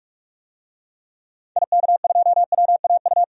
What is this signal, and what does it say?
Morse code, heard as a keyed tone, about 46 wpm, 700 Hz. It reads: IK2WAU
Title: DXCC-100 Call Signs - 6 Characters - 45wpm